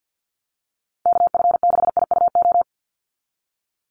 Morse code, heard as a keyed tone, about 50 wpm, 700 Hz. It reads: KF6IUO
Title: USA Call Signs 50wpm